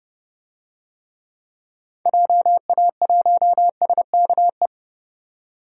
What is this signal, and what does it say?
Morse code, heard as a keyed tone, about 30 wpm, 700 Hz. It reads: JA1SKE
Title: DXCC-100 Call Signs - 6 Characters - 30wpm